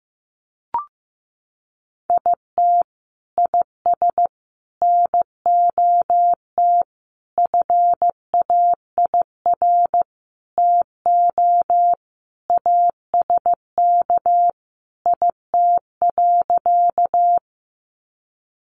Morse code, heard as a keyed tone, about 15 wpm, 700 Hz. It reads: IT IS NOT FAIR TO ASK IT.
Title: Sentences from Top 1300 Words 15wpm